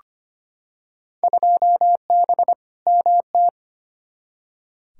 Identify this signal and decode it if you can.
Morse code, heard as a keyed tone, about 25 wpm, 700 Hz. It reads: 2B MT
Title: Field Day Exchanges Encore - 25wpm